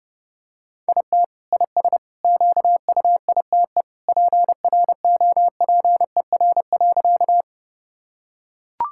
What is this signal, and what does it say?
Morse code, heard as a keyed tone, about 29 wpm, 700 Hz. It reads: IT IS QUITE PROPER.
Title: Sentences from Top 1000 Words - Review Part 5 - 30wpm